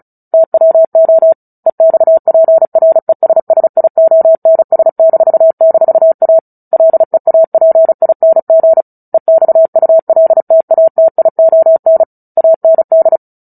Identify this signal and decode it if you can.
Morse code, heard as a keyed tone, about 34 wpm, 650 Hz. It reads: TWO EXPRESSIONS--A LEAPING EXULTATION AND
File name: Book Two / 09-Chapter 9 - 35 WPM